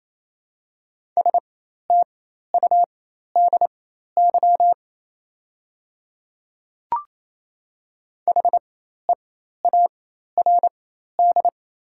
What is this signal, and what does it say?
Morse code, heard as a keyed tone, about 28 wpm, spaced as for 13 wpm, 700 Hz. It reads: STUDY HEARD
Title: 4x Character Spacing - Top 500 Words - 28wpm